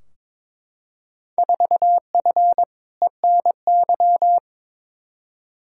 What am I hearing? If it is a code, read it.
Morse code, heard as a keyed tone, about 22 wpm, 700 Hz. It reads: 4F ENY
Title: Field Day Exchanges - 22wpm